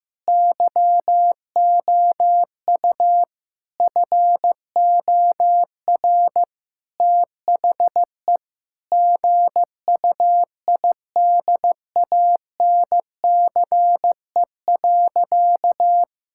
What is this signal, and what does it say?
Morse code, heard as a keyed tone, about 15 wpm, 700 Hz. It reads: YOU FOR THE GUIDANCE.
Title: Sentences from Top 1600 Words 15wpm